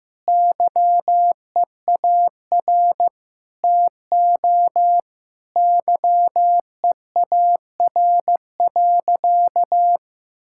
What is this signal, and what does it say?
Morse code, heard as a keyed tone, about 15 wpm, 700 Hz. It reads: YEAR TO YEAR.
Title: Sentences from Top 1600 Words 15wpm